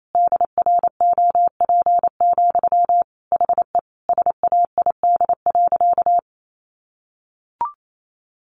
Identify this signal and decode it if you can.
Morse code, heard as a keyed tone, about 28 wpm, 700 Hz. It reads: DROP, HE SAID.
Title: Sentences from Top 1400 Words 28wpm